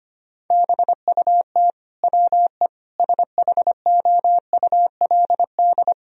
Morse code, heard as a keyed tone, about 25 wpm, 700 Hz. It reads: BUT WE SHOULD
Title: Sentences from Top 1000 Words - Review Part 4 - 25wpm